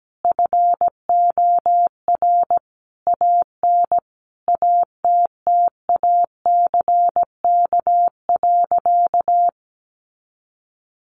Morse code, heard as a keyed tone, about 17 wpm, 700 Hz. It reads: FOR AN ATTACK.
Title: Sentences from Top 1500 Words 17wpm